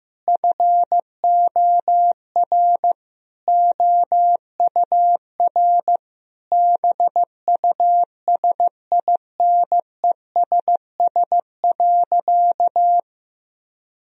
Morse code, heard as a keyed tone, about 15 wpm, 700 Hz. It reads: FOR OUR BUSINESS.
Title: Sentences from Top 1000 Words - Review Part 4 - 15wpm